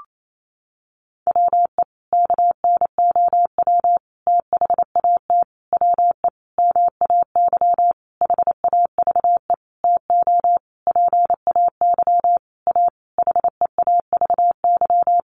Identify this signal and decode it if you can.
Morse code, heard as a keyed tone, about 28 wpm, 700 Hz. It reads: WE KNOW THAT WE MAY HAVE TO PAY A HEAVY
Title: Sentences from Top 1100 Words 28wpm